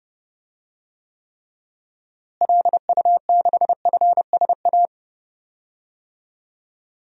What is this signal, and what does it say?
Morse code, heard as a keyed tone, about 32 wpm, 700 Hz. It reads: LU6FSA
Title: DXCC-100 Call Signs - 6 Characters - Repeated 3x - 30wpm